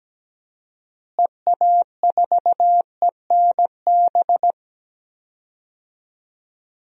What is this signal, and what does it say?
Morse code, heard as a keyed tone, about 17 wpm, 700 Hz. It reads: EA4ENB
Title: DXCC-100 Call Signs - 6 Characters - Repeated 3x - 17wpm